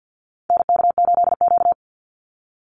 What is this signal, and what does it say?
Morse code, heard as a keyed tone, about 48 wpm, 700 Hz. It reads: NK8Q
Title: NA SOTA Activator Callsigns 50wpm